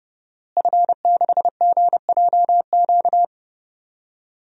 Morse code, heard as a keyed tone, about 30 wpm, 700 Hz. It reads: F6GJQ
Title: DXCC-100 Call Signs - 5 Characters - Repeated 2x - 30wpm